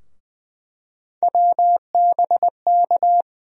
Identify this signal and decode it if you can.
Morse code, heard as a keyed tone, about 20 wpm, 700 Hz. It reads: WBK
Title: USA Call Sign Postfixes - 3 Letters 20wpm